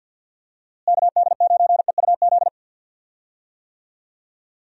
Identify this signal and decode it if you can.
Morse code, heard as a keyed tone, about 52 wpm, 700 Hz. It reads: KD9EUZ
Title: USA Call Signs - 6 Characters - Repeated 3x - 50wpm